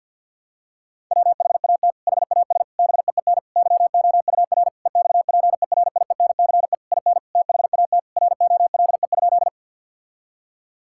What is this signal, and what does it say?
Morse code, heard as a keyed tone, about 52 wpm, 700 Hz. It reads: WHAT HAS BEEN YOUR EXPERIENCE IN THAT ROLE?
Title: Sentences from Top 1000 Words - Review Part 6 - 50wpm